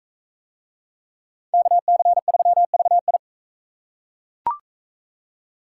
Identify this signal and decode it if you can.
Morse code, heard as a keyed tone, about 40 wpm, 700 Hz. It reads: KC3VI